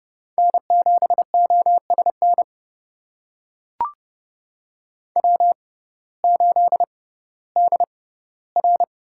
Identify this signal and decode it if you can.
Morse code, heard as a keyed tone, about 30 wpm, spaced as for 16 wpm, 700 Hz. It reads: N7OSN W8DR
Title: USA Call Signs - 5 Characters - 6x Character Spacing - 30wpm